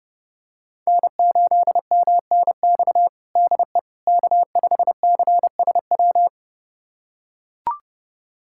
Morse code, heard as a keyed tone, about 30 wpm, 700 Hz. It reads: N8MNX DE K5CSW
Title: USA Callsign DE Callsign 30wpm